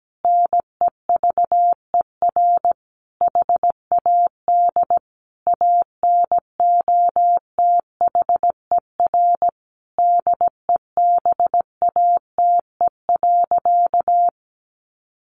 Morse code, heard as a keyed tone, about 17 wpm, 700 Hz. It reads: NEVER HAD ANOTHER DEBATE.
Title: Sentences from Top 1100 Words 17wpm